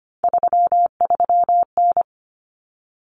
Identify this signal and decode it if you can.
Morse code, heard as a keyed tone, about 25 wpm, 700 Hz. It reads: 33N
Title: Common Cut Numbers - 3-Digit 25wpm